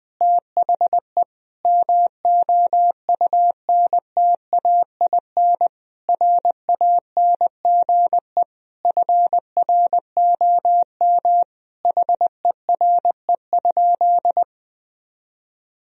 Morse code, heard as a keyed tone, about 20 wpm, 700 Hz. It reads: THE MOUNTAIN RANGE FROM HERE?
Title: Sentences from Top 1000 Words - Review Part 4 - 20wpm